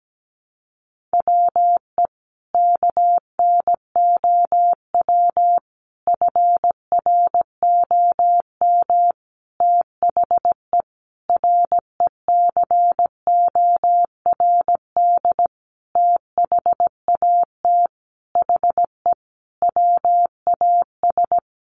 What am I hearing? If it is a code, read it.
Morse code, heard as a keyed tone, about 17 wpm, 700 Hz. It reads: WE KNOW FROM THE RECORD THAT HE WAS
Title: Sentences from Top 500 Words 17wpm